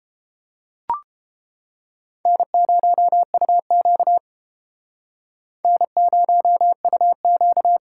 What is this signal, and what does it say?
Morse code, heard as a keyed tone, about 29 wpm, 700 Hz. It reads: N0UQ N0UQ